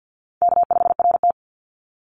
Morse code, heard as a keyed tone, about 50 wpm, 700 Hz. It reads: K5RT
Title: Call Signs - 50wpm